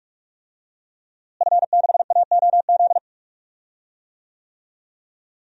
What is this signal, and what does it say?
Morse code, heard as a keyed tone, about 46 wpm, 700 Hz. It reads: F6AOZ
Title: DXCC-100 Call Signs - 5 Characters - Repeated 3x - 45wpm